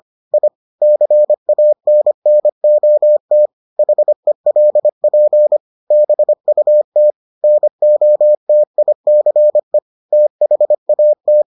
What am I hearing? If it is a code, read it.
Morse code, heard as a keyed tone, about 25 wpm, 600 Hz. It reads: I CANNOT HELP BUT NOTICE THAT